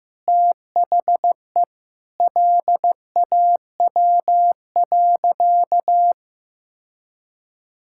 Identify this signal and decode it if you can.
Morse code, heard as a keyed tone, about 15 wpm, 700 Hz. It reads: THE LAW.
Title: Sentences from Top 1000 Words - Review Part 6 - 15wpm